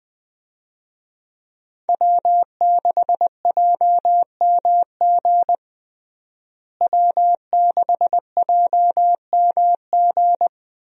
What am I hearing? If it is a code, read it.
Morse code, heard as a keyed tone, about 20 wpm, 700 Hz. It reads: W6JMG W6JMG
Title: USA Call Signs - 5 Characters - Repeated 2x - 20wpm